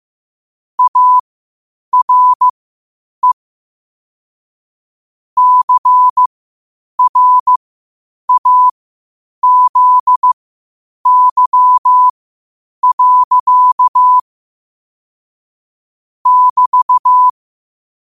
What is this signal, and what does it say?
Morse code, heard as a keyed tone, about 15 wpm, spaced as for 8 wpm, 1000 Hz. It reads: ARE CRAZY. <BT>
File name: qotdfiles / 8wpm